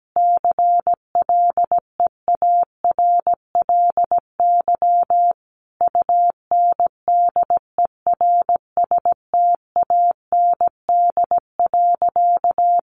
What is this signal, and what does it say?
Morse code, heard as a keyed tone, about 17 wpm, 700 Hz. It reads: CLEARLY UNDERSTAND.
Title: Sentences from Top 1300 Words 17wpm